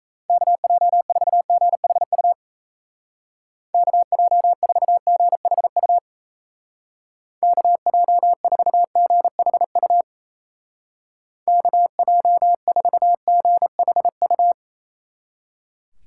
Morse code, heard as a keyed tone, about 35 wpm, 700 Hz. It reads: KJ4GHU KJ4GHU KJ4GHU KJ4GHU
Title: Speed Racing - USA 2x3 Call Signs 28wpm